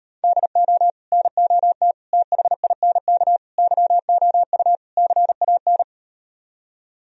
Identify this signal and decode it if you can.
Morse code, heard as a keyed tone, about 38 wpm, 700 Hz. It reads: DO NOT THINK YOU CAN